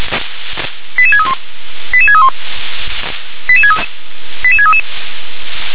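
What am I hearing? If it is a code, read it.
Five-tone selective call (ZVEI), heard as quick runs of five tones, tones of about 70 ms. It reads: 88631 88631 88633
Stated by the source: ZVEI five-tone call received by SDR.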